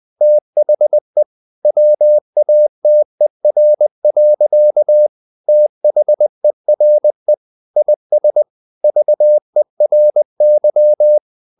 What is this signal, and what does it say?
Morse code, heard as a keyed tone, about 20 wpm, 600 Hz. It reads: THE WATER. THERE IS VERY
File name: the old man and the sea part 2 20wpm